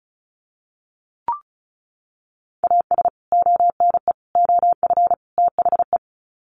Morse code, heard as a keyed tone, about 34 wpm, 700 Hz. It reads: AS ONE OF THE
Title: Sets of 4 Words 35wpm